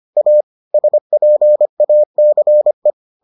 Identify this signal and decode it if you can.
Morse code, heard as a keyed tone, about 25 wpm, 600 Hz. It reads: A SPACE